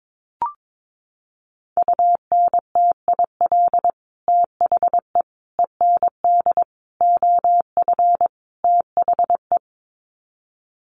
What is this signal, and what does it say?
Morse code, heard as a keyed tone, about 22 wpm, 700 Hz. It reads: UNTIL THE END OF THE
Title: Sets of 5 Words Encore 22wpm